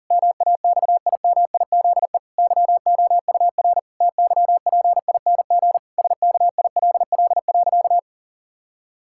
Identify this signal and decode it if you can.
Morse code, heard as a keyed tone, about 37 wpm, 700 Hz. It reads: MAXIMIZE YOUR TYPING SKILL.
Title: Pangram Challenge v2 - 40wpm